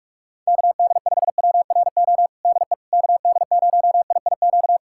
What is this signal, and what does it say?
Morse code, heard as a keyed tone, about 42 wpm, 700 Hz. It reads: KD5WRO DE KD0IIQ
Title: USA Callsign DE Callsign - 6 Characters - 45wpm